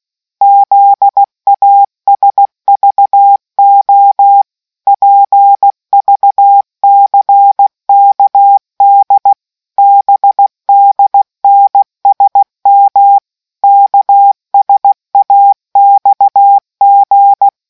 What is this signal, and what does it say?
Morse code, heard as a keyed tone, about 16 wpm, 800 Hz. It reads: ZASVO PVCKD BDNSM KSAXG